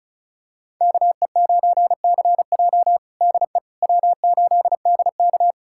Morse code, heard as a keyed tone, about 36 wpm, 700 Hz. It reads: KE9CJ DE W8DK
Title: USA Callsign DE Callsign 35wpm